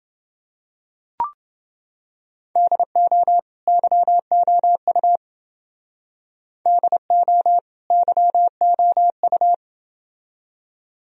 Morse code, heard as a keyed tone, about 29 wpm, 700 Hz. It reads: DO YOU DO YOU